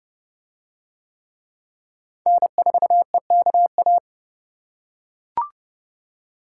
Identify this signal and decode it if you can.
Morse code, heard as a keyed tone, about 27 wpm, 700 Hz. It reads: N4EKA